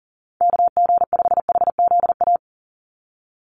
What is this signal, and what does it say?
Morse code, heard as a keyed tone, about 37 wpm, 700 Hz. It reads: KG5HZA
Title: USA Call Signs 40wpm